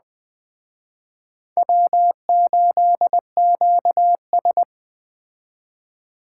Morse code, heard as a keyed tone, about 20 wpm, 700 Hz. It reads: W8QS
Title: Speed Racing - USA 1x2 Call Signs 17wpm